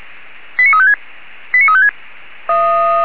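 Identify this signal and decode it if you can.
Five-tone selective call (ZVEI), heard as quick runs of five tones, tones of about 80 ms. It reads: 89356 89356
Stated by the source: ZVEI five-tone selective call received on an SDR by a German fire brigade.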